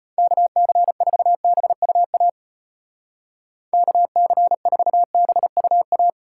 Morse code, heard as a keyed tone, about 37 wpm, 700 Hz. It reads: KC4BUA KC4BUA